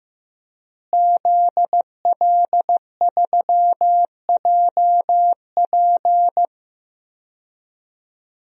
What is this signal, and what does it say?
Morse code, heard as a keyed tone, about 15 wpm, 700 Hz. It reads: ZL3JP
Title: DXCC-100 Call Signs - 5 Characters - Repeated 2x - 15wpm